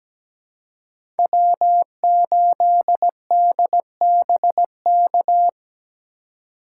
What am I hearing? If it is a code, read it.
Morse code, heard as a keyed tone, about 17 wpm, 700 Hz. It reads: W8DBK
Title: USA Call Signs - 5 Characters - 2x Character Spacing - 17wpm